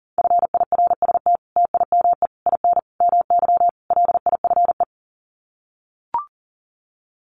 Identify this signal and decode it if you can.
Morse code, heard as a keyed tone, about 40 wpm, 700 Hz. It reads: FIRST TIME IN MY LIFE
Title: Sets of 5 Words Encore2 40wpm